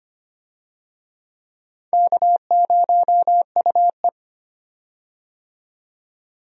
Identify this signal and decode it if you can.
Morse code, heard as a keyed tone, about 25 wpm, 700 Hz. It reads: K0UE
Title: USA Call Signs - 4 Characters - 25wpm